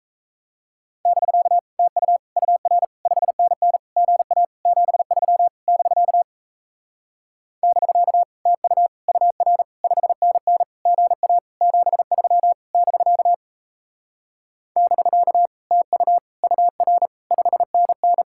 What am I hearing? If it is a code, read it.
Morse code, heard as a keyed tone, about 37 wpm, 700 Hz. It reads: <BK> TU UR 5NN GA 73 <BK> <BK> TU UR 5NN GA 73 <BK> <BK> TU UR 5NN